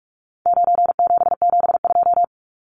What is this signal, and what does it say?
Morse code, heard as a keyed tone, about 44 wpm, 700 Hz. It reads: 9772
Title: Numbers - 4-Digit 45wpm